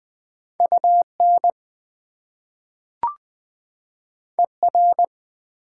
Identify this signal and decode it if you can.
Morse code, heard as a keyed tone, about 20 wpm, 700 Hz. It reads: UN ER
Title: Prefixes and Suffixes 20wpm